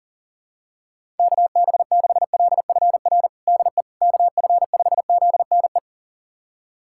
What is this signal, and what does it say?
Morse code, heard as a keyed tone, about 40 wpm, 700 Hz. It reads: KB6LFR DE KF5ZNE
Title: USA Callsign DE Callsign 40wpm